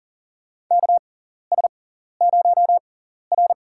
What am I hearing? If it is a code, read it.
Morse code, heard as a keyed tone, about 40 wpm, spaced as for 14 wpm, 700 Hz. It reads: KS0R
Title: USA Call Signs - 6x Character Spacing - 40wpm